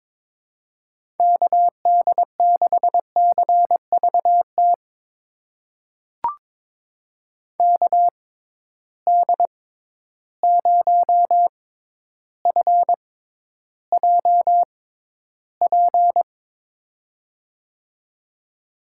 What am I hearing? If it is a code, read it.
Morse code, heard as a keyed tone, about 22 wpm, spaced as for 11 wpm, 700 Hz. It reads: KD6CVT KD0FJP
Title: USA Call Signs - 6 Characters - 6x Character Spacing - 22wpm